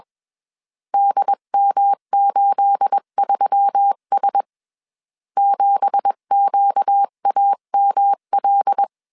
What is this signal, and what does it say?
Morse code, heard as a keyed tone, about 21 wpm, 800 Hz. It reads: DM83S 7QAML
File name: CW / DET / 1 9 1500C S B 20wpm 2313